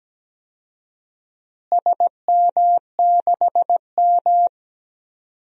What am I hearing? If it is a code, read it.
Morse code, heard as a keyed tone, about 17 wpm, 700 Hz. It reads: SM6M
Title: DXCC-100 Call Signs - 4 Characters - Repeated 2x - 17wpm